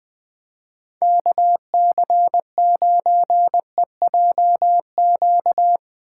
Morse code, heard as a keyed tone, about 23 wpm, 700 Hz. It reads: KC9EJQ
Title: Speed Racing - USA Call Signs 20wpm